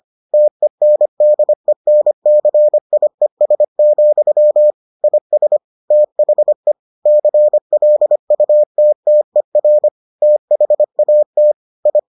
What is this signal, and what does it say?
Morse code, heard as a keyed tone, about 25 wpm, 600 Hz. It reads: TENDENCIES, IS THE CLUTTER THAT I